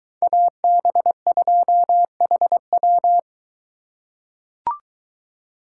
Morse code, heard as a keyed tone, about 23 wpm, 700 Hz. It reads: AB2HW